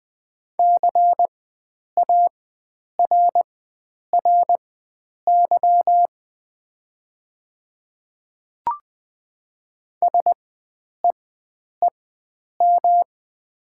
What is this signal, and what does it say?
Morse code, heard as a keyed tone, about 20 wpm, spaced as for 9 wpm, 700 Hz. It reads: CARRY SEEM
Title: 4x Character Spacing - Top 300 Words - 20wpm